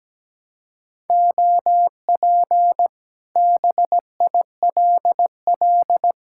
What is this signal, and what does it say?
Morse code, heard as a keyed tone, about 17 wpm, 700 Hz. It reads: OP BILL
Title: Speed-Racing Common QSO Phrases - 17wpm